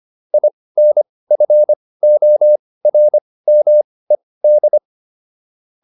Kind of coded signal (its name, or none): Morse code